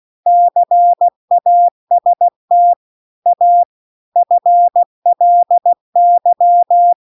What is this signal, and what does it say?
Morse code, heard as a keyed tone, about 16 wpm, 700 Hz. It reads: CAST A FLY
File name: CW-The-Big-Twoheaded-River-2-16WPM